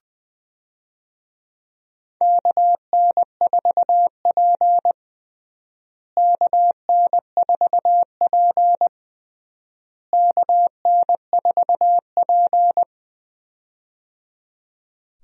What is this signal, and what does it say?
Morse code, heard as a keyed tone, about 21 wpm, 700 Hz. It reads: KN4P KN4P KN4P
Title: USA Call Signs - 4 Characters - Repeated 3x - 20wpm